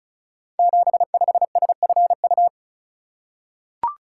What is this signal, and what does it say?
Morse code, heard as a keyed tone, about 36 wpm, 700 Hz. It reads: 75SFU
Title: Code Groups - 5 Letter:Numbers - 35wpm